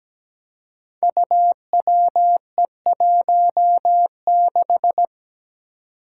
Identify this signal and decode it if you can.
Morse code, heard as a keyed tone, about 17 wpm, 700 Hz. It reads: UWE16
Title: Five Challenging Letter-Number Combinations 17wpm